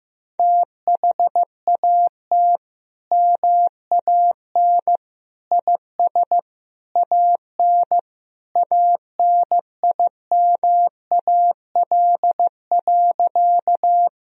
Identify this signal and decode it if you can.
Morse code, heard as a keyed tone, about 15 wpm, 700 Hz. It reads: THAT MAN IS AN ANIMAL.
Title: Sentences from Top 1000 Words - Review Part 1 - 15wpm